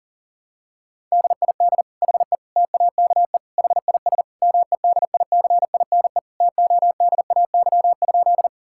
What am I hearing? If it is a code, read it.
Morse code, heard as a keyed tone, about 40 wpm, 700 Hz. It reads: DID HE TAKE HIS MEDICINE TODAY?
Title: Sentences from Top 1000 Words - Review Part 6 - 40wpm